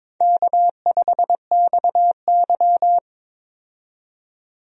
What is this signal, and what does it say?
Morse code, heard as a keyed tone, about 22 wpm, 700 Hz. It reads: K5XY
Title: USA Call Signs - 4 Characters - 22wpm